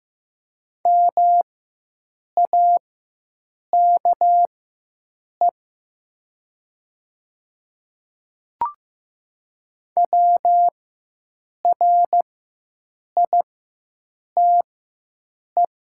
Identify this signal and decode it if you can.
Morse code, heard as a keyed tone, about 15 wpm, spaced as for 7 wpm, 700 Hz. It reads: MAKE WRITE
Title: 4x Character Spacing - Top 100 Words - 15wpm